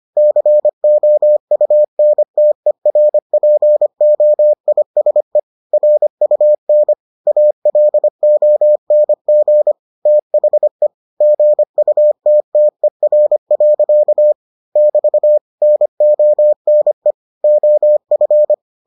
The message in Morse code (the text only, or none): COUNTERPOISE RUN ALONG THE GUTTER. <BT> NONE OF